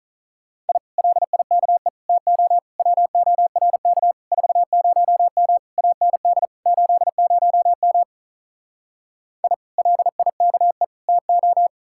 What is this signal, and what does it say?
Morse code, heard as a keyed tone, about 46 wpm, 700 Hz. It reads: I LIKE TO WORK 40M AND 80M I LIKE TO